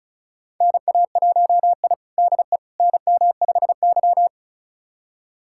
Morse code, heard as a keyed tone, about 36 wpm, 700 Hz. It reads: NA1I DE NM5Y
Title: USA Callsign DE Callsign - 4 Characters - 35wpm